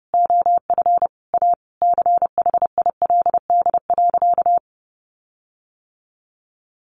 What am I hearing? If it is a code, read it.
Morse code, heard as a keyed tone, about 30 wpm, 700 Hz. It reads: OF A CHILD.
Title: Sentences from Top 1400 Words 30wpm